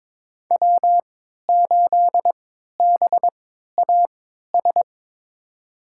Morse code, heard as a keyed tone, about 22 wpm, spaced as for 12 wpm, 700 Hz. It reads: W8BAS
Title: USA Call Signs - 5 Characters - 3x Character Spacing - 22wpm